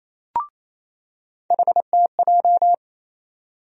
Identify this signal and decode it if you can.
Morse code, heard as a keyed tone, about 28 wpm, 700 Hz. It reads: HTJ
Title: USA Call Sign Postfixes - 3 Letters 28wpm